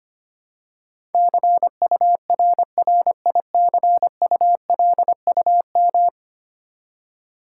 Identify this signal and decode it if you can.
Morse code, heard as a keyed tone, about 25 wpm, 700 Hz. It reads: CURRICULUM
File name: 10 Letter Words - First 3 Letters Given - 25wpm